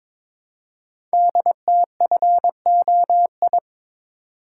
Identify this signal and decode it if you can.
Morse code, heard as a keyed tone, about 22 wpm, 700 Hz. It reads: DTFOI
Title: Code Groups - 5 Letters - 22wpm